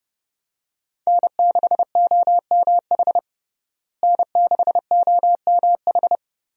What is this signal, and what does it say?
Morse code, heard as a keyed tone, about 30 wpm, 700 Hz. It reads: N6OMH N6OMH
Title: USA Call Signs - 5 Characters - Repeated 3x - 30wpm